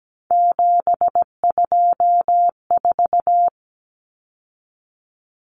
Morse code, heard as a keyed tone, about 17 wpm, 700 Hz. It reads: 724
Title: Numbers - 3-Digit 17wpm